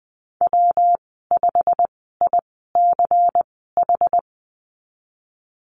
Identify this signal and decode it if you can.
Morse code, heard as a keyed tone, about 20 wpm, spaced as for 15 wpm, 700 Hz. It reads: W5ICH
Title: 161 Lesson 26g - Letter g - Call Signs 20wpm